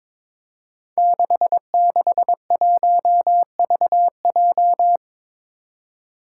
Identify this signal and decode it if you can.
Morse code, heard as a keyed tone, about 22 wpm, 700 Hz. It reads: 661VJ
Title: Five Challenging Letter-Number Combinations 22wpm